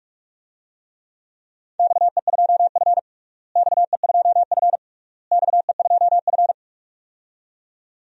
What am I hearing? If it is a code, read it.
Morse code, heard as a keyed tone, about 46 wpm, 700 Hz. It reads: XE2F XE2F XE2F
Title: DXCC-100 Call Signs - 4 Characters - Repeated 3x - 45wpm